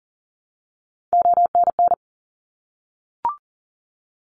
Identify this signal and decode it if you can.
Morse code, heard as a keyed tone, about 41 wpm, 700 Hz. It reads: ONN
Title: ARRL Sections 40wpm